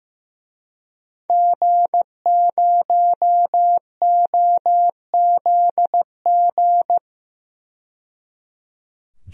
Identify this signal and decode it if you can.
Morse code, heard as a keyed tone, about 15 wpm, 700 Hz. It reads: G0OZG
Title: DXCC-100 Call Signs - 5 Characters - 15wpm